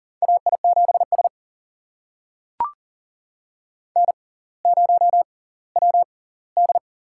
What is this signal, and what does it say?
Morse code, heard as a keyed tone, about 39 wpm, spaced as for 20 wpm, 700 Hz. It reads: AI7S N0WD
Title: USA Call Signs - 4 Characters - 6x Character Spacing - 40wpm